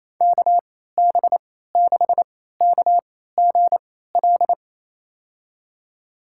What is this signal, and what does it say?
Morse code, heard as a keyed tone, about 28 wpm, spaced as for 16 wpm, 700 Hz. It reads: KB6KGL
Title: USA Call Signs - 3x Character Spacing - 28wpm